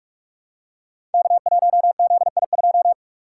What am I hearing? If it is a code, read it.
Morse code, heard as a keyed tone, about 44 wpm, 700 Hz. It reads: K1ZI2
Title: Code Groups - 5 Letter:Numbers - 45wpm